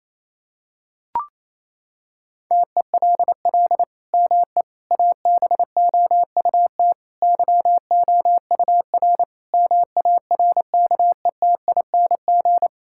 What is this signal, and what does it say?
Morse code, heard as a keyed tone, about 28 wpm, 700 Hz. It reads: TELL ME ABOUT YOUR MARKETING
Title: Sentences from Top 1000 Words - Review Part 5 - 28wpm